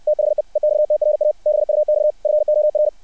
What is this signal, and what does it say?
Morse code, heard as a keyed tone, about 60 wpm, 600 Hz. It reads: THE EXTRA BUX FOR
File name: archived / CRSnet